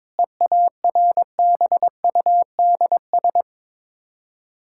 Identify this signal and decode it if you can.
Morse code, heard as a keyed tone, about 22 wpm, 700 Hz. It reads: EARBUDS
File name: Difficult Words with Hints - 22wpm